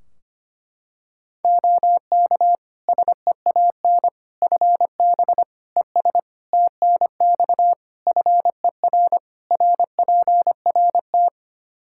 Morse code, heard as a keyed tone, about 25 wpm, 700 Hz. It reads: OK SEAN FB ES TNX FER RPRT
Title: Practice QSO - Exchange 2 - Info - 25wpm